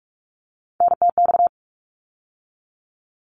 Morse code, heard as a keyed tone, about 46 wpm, 700 Hz. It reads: NTX
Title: ARRL Sections 45wpm